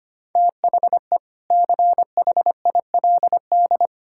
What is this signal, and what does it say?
Morse code, heard as a keyed tone, about 25 wpm, 700 Hz. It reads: THE CHILD
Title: Sentences from Top 1000 Words - Review Part 6 - 25wpm